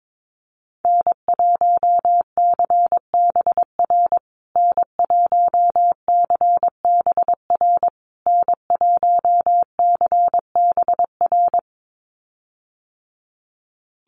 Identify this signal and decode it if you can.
Morse code, heard as a keyed tone, about 22 wpm, 700 Hz. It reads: N1CBR N1CBR N1CBR
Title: USA Call Signs Repeated 3x 22wpm